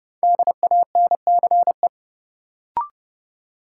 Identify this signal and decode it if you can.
Morse code, heard as a keyed tone, about 30 wpm, 700 Hz. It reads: DANCE
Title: Sentences from Top 700 Words - New Word Drill - 30wpm